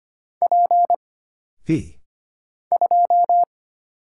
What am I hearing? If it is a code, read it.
Morse code, heard as a keyed tone, about 25 wpm, 700 Hz. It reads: P 2
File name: Single Character (Letter:Number:Punctuation:Prosign) - Mind-Melt - 25wpm